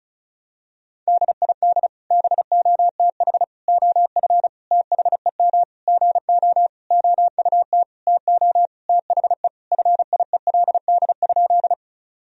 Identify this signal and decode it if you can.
Morse code, heard as a keyed tone, about 34 wpm, 700 Hz. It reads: DID BOTH OF THEM GO OUT TO THE FIELD?
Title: Sentences from Top 1000 Words - Review Part 2 - 35wpm